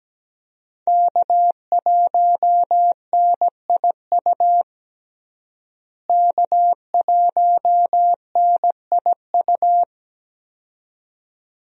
Morse code, heard as a keyed tone, about 17 wpm, 700 Hz. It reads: K1NIU K1NIU
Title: USA Call Signs - 5 Characters - Repeated 2x - 17wpm